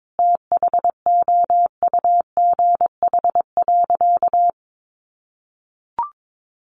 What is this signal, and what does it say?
Morse code, heard as a keyed tone, about 22 wpm, 700 Hz. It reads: THOUGH.
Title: Sentences from Top 1400 Words 22wpm